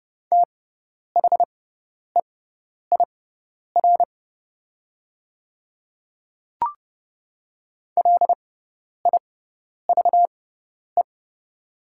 Morse code, heard as a keyed tone, about 30 wpm, spaced as for 10 wpm, 690 Hz. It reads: THEIR LIVE
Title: 6x Character Spacing - Top 200 Words - 30wpm